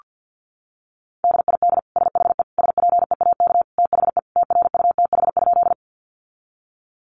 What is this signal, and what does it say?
Morse code, heard as a keyed tone, about 48 wpm, 700 Hz. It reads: DID SHE SPEAK THE TRUTH?
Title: Sentences from Top 1100 Words 50wpm